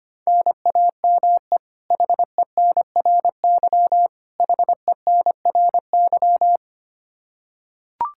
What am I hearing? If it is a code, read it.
Morse code, heard as a keyed tone, about 25 wpm, 700 Hz. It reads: NAME HENRY HENRY
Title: Practice QSO - Exchange 1 - Beginning - 25wpm